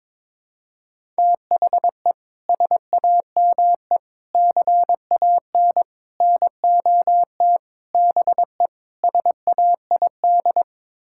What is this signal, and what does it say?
Morse code, heard as a keyed tone, about 22 wpm, 700 Hz. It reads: THE SAME CAN NOT BE SAID
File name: practice / 22wpm / Sets of 6 Words - 22wpm